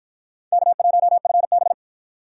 Morse code, heard as a keyed tone, about 53 wpm, 700 Hz. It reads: K1FB